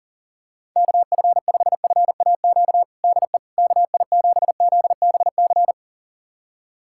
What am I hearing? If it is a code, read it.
Morse code, heard as a keyed tone, about 40 wpm, 700 Hz. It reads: KF5FAQ DE KI7ZBC
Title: USA Callsign DE Callsign 40wpm